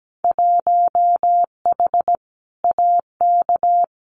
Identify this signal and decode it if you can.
Morse code, heard as a keyed tone, about 17 wpm, 700 Hz. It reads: 1H AK
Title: Winter Field Day Exchanges Encore 17wpm